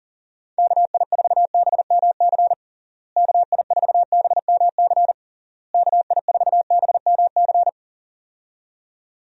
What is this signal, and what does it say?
Morse code, heard as a keyed tone, about 40 wpm, 700 Hz. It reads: KI4BMC KI4BMC KI4BMC
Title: USA Call Signs - 6 Characters - Repeated 3x - 40wpm